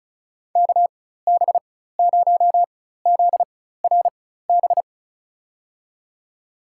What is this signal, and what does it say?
Morse code, heard as a keyed tone, about 36 wpm, spaced as for 16 wpm, 700 Hz. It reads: KB0ZRB
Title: USA Call Signs - 6 Characters - 4x Character Spacing - 35wpm